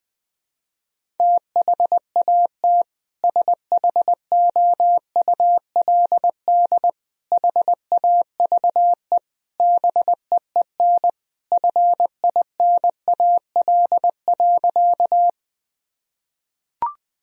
Morse code, heard as a keyed tone, about 20 wpm, 700 Hz. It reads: THAT SHOULD HAVE BEEN FINAL.
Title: Sentences from Top 1000 Words - Review Part 2 - 20wpm